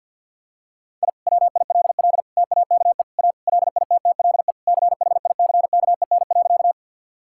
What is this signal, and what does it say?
Morse code, heard as a keyed tone, about 49 wpm, 700 Hz. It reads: I WILL TAKE A LITTLE CHICKEN.
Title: Sentences from Top 1000 Words - Review Part 5 - 50wpm